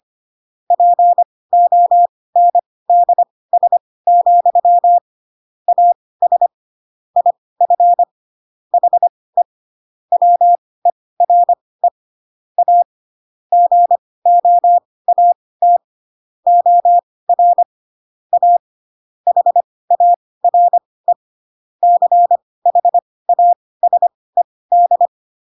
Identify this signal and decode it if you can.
Morse code, heard as a keyed tone, about 25 wpm, spaced as for 18 wpm, 700 Hz. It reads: PONDS, AS IF HE WERE A GOAT OR A HARE CHASED
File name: Pinocchio / Pinocchio ch4 12 25 ik1hge